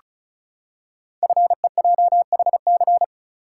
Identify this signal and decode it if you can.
Morse code, heard as a keyed tone, about 36 wpm, 700 Hz. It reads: FEJHC
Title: Code Groups - 5 Letters - 35wpm